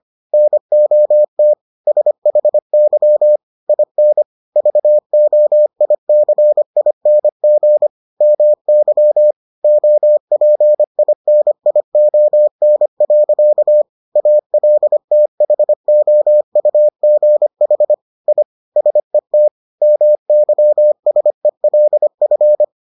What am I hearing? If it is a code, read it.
Morse code, heard as a keyed tone, about 25 wpm, 600 Hz. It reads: NOT SHY IN VOICING MY OPINION. ALTHOUGH I SET MYSELF